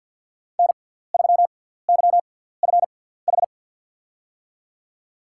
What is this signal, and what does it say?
Morse code, heard as a keyed tone, about 50 wpm, spaced as for 17 wpm, 690 Hz. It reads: N3YFH
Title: USA Call Signs - 5 Characters - 6x Character Spacing - 50wpm